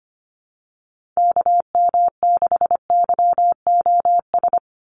Morse code, heard as a keyed tone, about 25 wpm, 700 Hz. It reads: KM6YOS
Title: USA Call Signs 25wpm